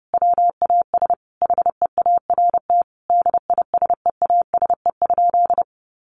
Morse code, heard as a keyed tone, about 30 wpm, 700 Hz. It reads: WAS HEART DISEASE?
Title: Sentences from Top 1200 Words 30wpm